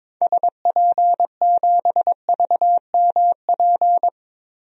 Morse code, heard as a keyed tone, about 22 wpm, 700 Hz. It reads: SP7VMP
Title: DXCC-100 Call Signs - 6 Characters - Repeated 3x - 22wpm